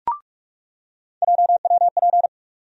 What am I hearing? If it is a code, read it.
Morse code, heard as a keyed tone, about 46 wpm, 700 Hz. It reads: JWP
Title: USA Call Sign Postfixes - 3 Letters 45wpm